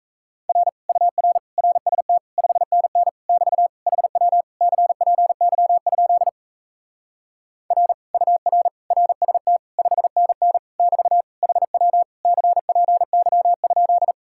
Morse code, heard as a keyed tone, about 37 wpm, 700 Hz. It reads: R UR RST 5NN <BT> HW CPY? R UR RST 5NN <BT> HW CPY?